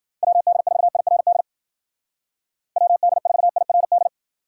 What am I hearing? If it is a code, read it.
Morse code, heard as a keyed tone, about 53 wpm, 700 Hz. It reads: WD4IRD WD4IRD